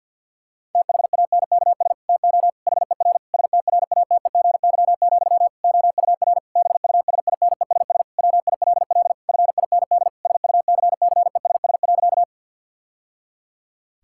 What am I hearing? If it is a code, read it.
Morse code, heard as a keyed tone, about 52 wpm, 700 Hz. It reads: THANKS TO HER STRATEGY, OUR BUSINESS WILL FIND SUCCESS.
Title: Sentences from Top 1000 Words - Review Part 5 - 50wpm